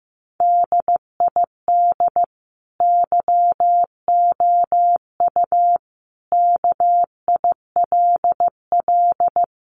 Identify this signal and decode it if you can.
Morse code, heard as a keyed tone, about 15 wpm, 700 Hz. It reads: DID YOU KILL
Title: Sentences from Top 1400 Words 15wpm